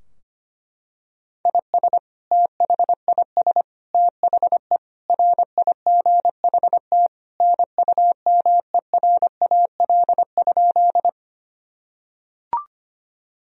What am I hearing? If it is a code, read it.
Morse code, heard as a keyed tone, about 25 wpm, 700 Hz. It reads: IS THIS THE RIGHT NUMERAL?
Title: Sentences from Top 1000 Words - Review Part 2 - 25wpm